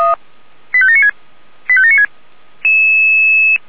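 Five-tone selective call (ZVEI), heard as quick runs of five tones, tones of about 80 ms. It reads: 85796 85796
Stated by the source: ZVEI five-tone selective call received by SDR.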